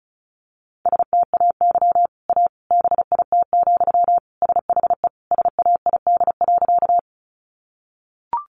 Morse code, heard as a keyed tone, about 34 wpm, 700 Hz. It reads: STAY A BIT, SHE SAID.
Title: Sentences from Top 900 Words 35wpm